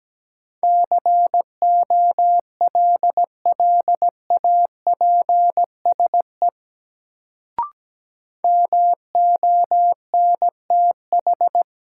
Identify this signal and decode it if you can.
Morse code, heard as a keyed tone, about 17 wpm, 700 Hz. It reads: COLLAPSE MONTH
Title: Sets of 5 Words Encore - New Word Drill - 17wpm